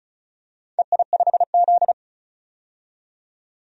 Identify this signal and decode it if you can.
Morse code, heard as a keyed tone, about 36 wpm, 700 Hz. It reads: EI5Z
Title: DXCC-100 Call Signs - 4 Characters - 35wpm